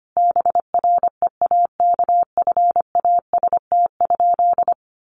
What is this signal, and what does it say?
Morse code, heard as a keyed tone, about 25 wpm, 700 Hz. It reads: BREAKFAST?
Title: Sentences from Top 1500 Words 25wpm